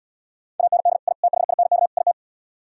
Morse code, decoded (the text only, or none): AND I THINK IT